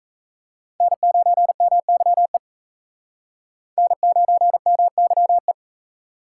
Morse code, heard as a keyed tone, about 36 wpm, 700 Hz. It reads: N9MYE N9MYE